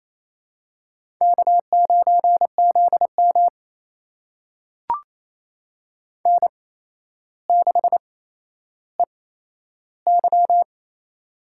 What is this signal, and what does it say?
Morse code, heard as a keyed tone, about 28 wpm, spaced as for 12 wpm, 700 Hz. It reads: K9ZM N6EY
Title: USA Call Signs - 4 Characters - 8x Character Spacing - 28wpm